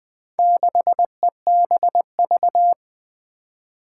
Morse code, heard as a keyed tone, about 20 wpm, 700 Hz. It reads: 6EBV
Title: Four Challenging Letter-Number Combinations 20wpm